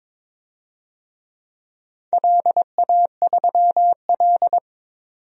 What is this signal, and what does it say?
Morse code, heard as a keyed tone, about 22 wpm, 700 Hz. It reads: LA3L
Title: DXCC-100 Call Signs - 4 Characters - 22wpm